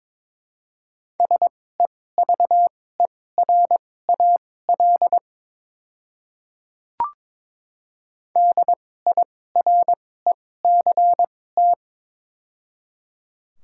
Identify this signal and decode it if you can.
Morse code, heard as a keyed tone, about 22 wpm, spaced as for 16 wpm, 700 Hz. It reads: SEVERAL DIRECT
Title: 2x Character Spacing - Top 500 Words - 22wpm